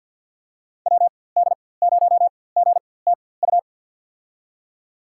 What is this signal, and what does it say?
Morse code, heard as a keyed tone, about 47 wpm, spaced as for 23 wpm, 700 Hz. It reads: WD0GTU
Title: USA Call Signs - 6 Characters - 4x Character Spacing - 50wpm